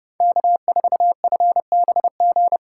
Morse code, heard as a keyed tone, about 30 wpm, 690 Hz. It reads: K4FBG
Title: USA Call Signs - 3x Character Spacing - 30wpm